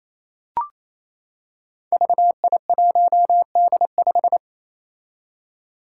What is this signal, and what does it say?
Morse code, heard as a keyed tone, about 28 wpm, 700 Hz. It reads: VI1D5
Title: Five Challenging Letter-Number Combinations 28wpm